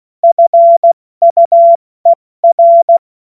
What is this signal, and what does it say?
Morse code, heard as a keyed tone, about 16 wpm, 680 Hz. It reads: FUER
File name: Morsekurs / Witze / Witz 07. Juni 2022 70 F80